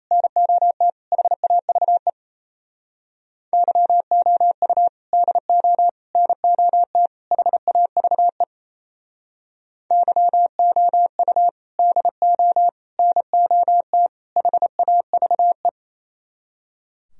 Morse code, heard as a keyed tone, about 36 wpm, 700 Hz. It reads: NOT HAVE YOU DO NOT HAVE YOU DO NOT HAVE